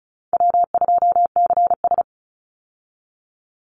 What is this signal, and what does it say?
Morse code, heard as a keyed tone, about 36 wpm, 700 Hz. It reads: W2CS
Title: Call Signs - 35wpm